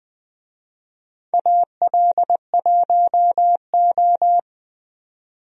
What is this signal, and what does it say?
Morse code, heard as a keyed tone, about 20 wpm, 700 Hz. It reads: AL1O
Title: USA Call Signs - 4 Characters - 20wpm